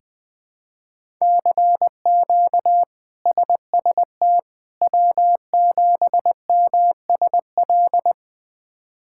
Morse code, heard as a keyed tone, about 21 wpm, 700 Hz. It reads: CQ SST W7MSL
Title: K1USN SST QSO Practice - 20wpm